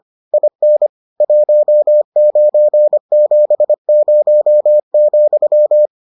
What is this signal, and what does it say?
Morse code, heard as a keyed tone, about 25 wpm, 600 Hz. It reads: IN 1970,